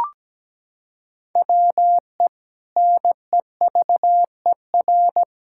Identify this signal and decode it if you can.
Morse code, heard as a keyed tone, about 17 wpm, 700 Hz. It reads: WE NEVER
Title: Sentences from Top 1600 Words 17wpm